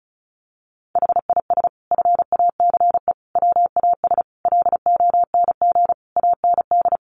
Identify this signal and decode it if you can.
Morse code, heard as a keyed tone, about 34 wpm, 700 Hz. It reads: HIS FACE WAS LONG AND
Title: Sentences from Top 1200 Words 35wpm